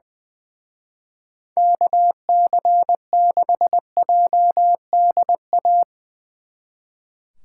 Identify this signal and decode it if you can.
Morse code, heard as a keyed tone, about 21 wpm, 700 Hz. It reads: KC6JDA
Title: Speed Racing - USA 2x3 Call Signs 20wpm